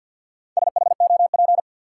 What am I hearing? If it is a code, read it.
Morse code, heard as a keyed tone, about 52 wpm, 700 Hz. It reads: SHOP
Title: Sentences from Top 1600 Words - New Word Drill - 50wpm